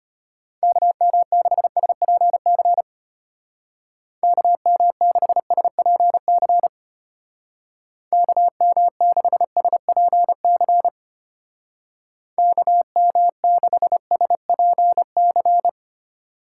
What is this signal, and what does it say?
Morse code, heard as a keyed tone, about 30 wpm, 700 Hz. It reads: KM6SPC KM6SPC KM6SPC KM6SPC
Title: Speed Racing - USA Call Signs 25wpm